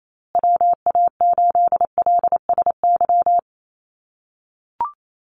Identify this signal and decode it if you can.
Morse code, heard as a keyed tone, about 28 wpm, 700 Hz. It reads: WA8LSY
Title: USA Call Signs Repeated 3x 28wpm